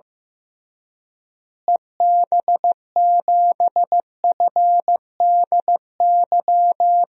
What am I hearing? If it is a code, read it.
Morse code, heard as a keyed tone, about 15 wpm, 700 Hz. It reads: EB7FDY
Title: DXCC-100 Call Signs - 6 Characters - Repeated 3x - 15wpm